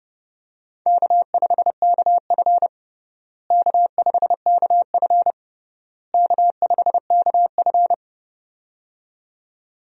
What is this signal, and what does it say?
Morse code, heard as a keyed tone, about 30 wpm, 700 Hz. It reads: K5KF K5KF K5KF
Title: USA Call Signs - 4 Characters - Repeated 3x - 30wpm